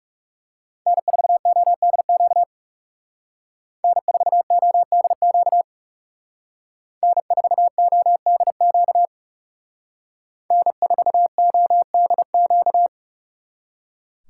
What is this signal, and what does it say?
Morse code, heard as a keyed tone, about 38 wpm, 700 Hz. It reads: N4ODQ N4ODQ N4ODQ N4ODQ